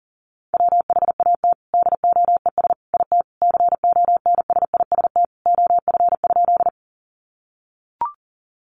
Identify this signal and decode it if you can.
Morse code, heard as a keyed tone, about 40 wpm, 700 Hz. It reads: WHAT DOES IT CONSIST OF?
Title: Sentences from Top 1400 Words 40wpm